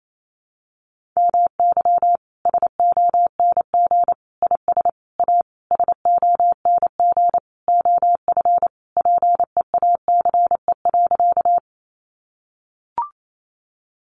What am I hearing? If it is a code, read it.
Morse code, heard as a keyed tone, about 28 wpm, 700 Hz. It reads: MY SONG IS A SONG OF PEACE.
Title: Sentences from Top 1500 Words 28wpm